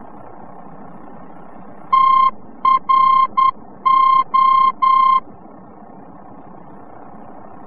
Morse code, heard as a keyed tone, about 10 wpm, 1060 Hz. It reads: TRO
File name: france / mx / TRO-VOR-F